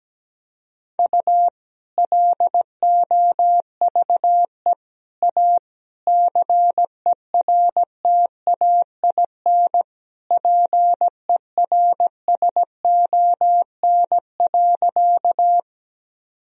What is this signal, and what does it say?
Morse code, heard as a keyed tone, about 17 wpm, 700 Hz. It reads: U LOVE A CERTAIN PERSON.
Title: Sentences from Top 500 Words and Abbreviations 17wpm